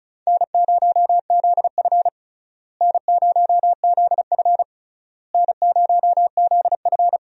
Morse code, heard as a keyed tone, about 34 wpm, 700 Hz. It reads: N0ZF N0ZF N0ZF
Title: USA Call Signs - 4 Characters - Repeated 3x - 35wpm